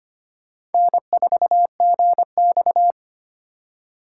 Morse code, heard as a keyed tone, about 31 wpm, 700 Hz. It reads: N4GX
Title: Speed Racing - USA 1x2 Call Signs 25wpm